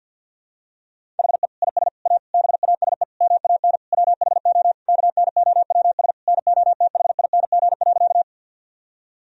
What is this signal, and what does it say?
Morse code, heard as a keyed tone, about 47 wpm, 700 Hz. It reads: HE IS A BASE MAN WHO KNOWS NOTHING.
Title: Sentences from Top 1000 Words - Review Part 1 - 50wpm